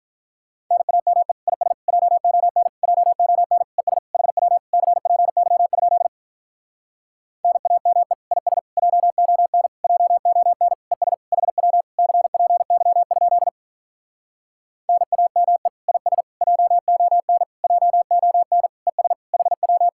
Morse code, heard as a keyed tone, about 46 wpm, 700 Hz. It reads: NAME IS JON JON ES HW CPY? NAME IS JON JON ES HW CPY? NAME IS JON JON ES HW